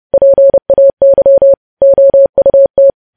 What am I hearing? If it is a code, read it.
Morse code, heard as a keyed tone, about 30 wpm, 550 Hz. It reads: PAY OUT
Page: news headlines in morse code at 30 wpm